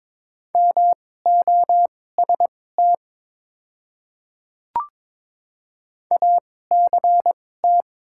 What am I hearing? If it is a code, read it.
Morse code, heard as a keyed tone, about 22 wpm, spaced as for 16 wpm, 700 Hz. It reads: MOST ACT
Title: 2x Character Spacing - Top 200 Words - 22wpm